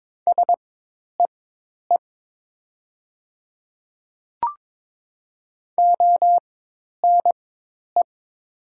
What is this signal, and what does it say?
Morse code, heard as a keyed tone, about 22 wpm, spaced as for 10 wpm, 700 Hz. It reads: SEE ONE
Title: 4x Character Spacing - Top 100 Words - 22wpm